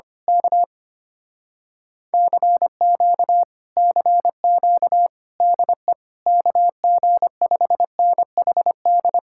Morse code, heard as a keyed tone, about 24 wpm, 700 Hz. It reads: K CQ CQ DE KG5NHD
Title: Speed-Racing Common QSO Phrases Encore - 25wpm